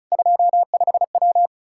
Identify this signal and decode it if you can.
Morse code, heard as a keyed tone, about 36 wpm, 700 Hz. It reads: 25W
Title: K1USN SST QSO Practice 35wpm